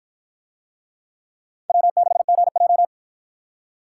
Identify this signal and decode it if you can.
Morse code, heard as a keyed tone, about 53 wpm, 700 Hz. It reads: W6GJ